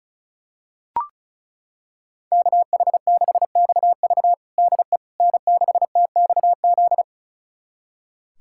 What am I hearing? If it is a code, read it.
Morse code, heard as a keyed tone, about 36 wpm, 700 Hz. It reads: KH6XV DE N6TXZ
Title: USA Callsign DE Callsign - 5 Characters - 35wpm